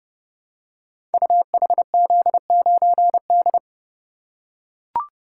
Morse code, heard as a keyed tone, about 30 wpm, 700 Hz. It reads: UHZ9D
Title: Code Groups - 5 Letter:Numbers - 30wpm